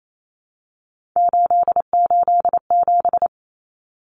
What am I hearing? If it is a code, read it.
Morse code, heard as a keyed tone, about 28 wpm, 700 Hz. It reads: 887
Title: Common Cut Numbers - 3-Digit 28wpm